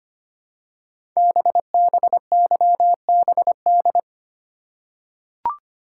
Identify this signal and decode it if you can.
Morse code, heard as a keyed tone, about 25 wpm, 700 Hz. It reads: BBYBD
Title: Code Groups - 5 Letters - 25wpm